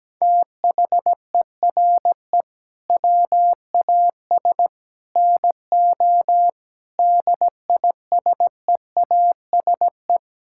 Morse code, heard as a keyed tone, about 17 wpm, 700 Hz. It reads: THERE WAS NO DISEASE
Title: Sentences from Top 1000 Words - Review Part 6 - 17wpm